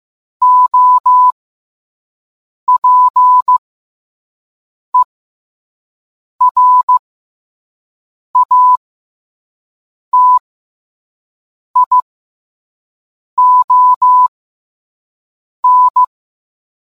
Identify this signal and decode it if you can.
Morse code, heard as a keyed tone, about 15 wpm, spaced as for 5 wpm, 1000 Hz. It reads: OPERATION